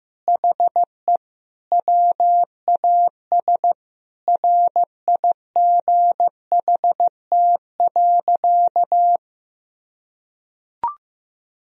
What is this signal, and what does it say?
Morse code, heard as a keyed tone, about 15 wpm, 700 Hz. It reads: HE WAS RIGHT.
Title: Sentences from Top 1000 Words - Review Part 4 - 15wpm